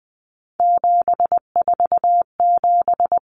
Morse code, heard as a keyed tone, about 20 wpm, 700 Hz. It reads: 747
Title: Numbers - 3-Digit 20wpm